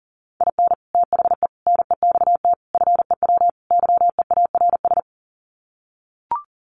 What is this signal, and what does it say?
Morse code, heard as a keyed tone, about 40 wpm, 700 Hz. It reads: IN THE NEXT FEW YEARS
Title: Sets of 5 Words Encore 40wpm